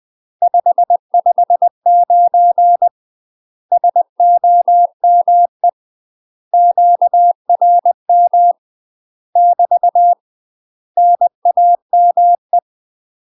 Morse code, heard as a keyed tone, about 20 wpm, 700 Hz. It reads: 559 SOME QRM <BT> NAME
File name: UK MASTER QSOs TRAINING / 20WPM / UK MASTER QSOs 0020